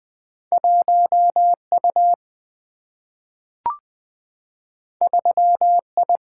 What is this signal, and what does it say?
Morse code, heard as a keyed tone, about 20 wpm, 700 Hz. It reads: 1U 3I
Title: Single-Letter Single-Number 20wpm